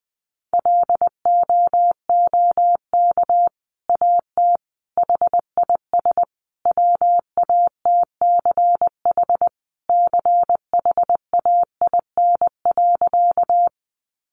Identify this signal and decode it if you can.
Morse code, heard as a keyed tone, about 20 wpm, 700 Hz. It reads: LOOK AT HIS WATCH CHAIN.
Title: Sentences from Top 1500 Words 20wpm